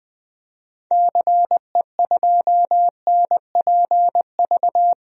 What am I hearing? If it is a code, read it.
Morse code, heard as a keyed tone, about 20 wpm, 700 Hz. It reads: CE2NPV
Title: DXCC-100 Call Signs - 6 Characters - 20wpm